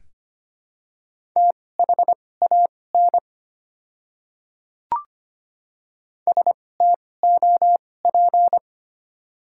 Morse code, heard as a keyed tone, about 25 wpm, spaced as for 18 wpm, 700 Hz. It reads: THAN STOP
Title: 2x Character Spacing - Top 300 Words - 25wpm